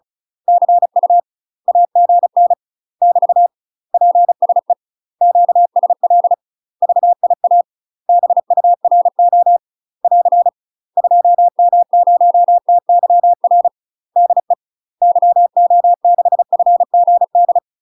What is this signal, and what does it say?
Morse code, heard as a keyed tone, about 34 wpm, 700 Hz. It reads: CU AGN <BT> PSE QSL VIA BURO <AR> 2M0TYR DE YO6FGD
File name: UK MASTER QSOs TRAINING / 35WPM / UK MASTER QSOs 0042